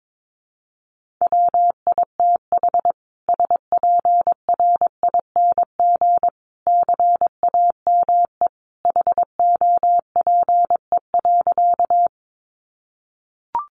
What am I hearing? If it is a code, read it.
Morse code, heard as a keyed tone, about 22 wpm, 700 Hz. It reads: WITH SPRING CAME HOPE.
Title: Sentences from Top 1200 Words 22wpm